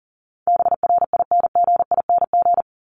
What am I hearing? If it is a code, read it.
Morse code, heard as a keyed tone, about 40 wpm, 700 Hz. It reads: BRINGING
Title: ING Suffix - Top 100 Words with Base Words 40wpm